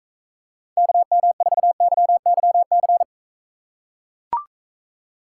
Morse code, heard as a keyed tone, about 44 wpm, 700 Hz. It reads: KM4YYC